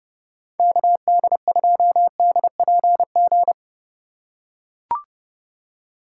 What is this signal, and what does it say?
Morse code, heard as a keyed tone, about 31 wpm, 700 Hz. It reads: KD2DPG
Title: USA Call Signs - 6 Characters - 2x Character Spacing - 30wpm